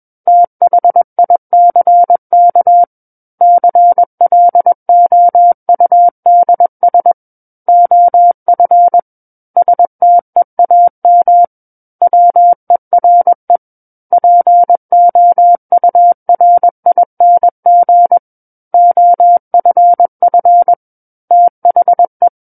Morse code, heard as a keyed tone, about 21 wpm, 700 Hz. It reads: THICK CLOUDS OF STEAM WERE POURING OFF THE
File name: Book One / 12-Chapter 12 - 21 WPM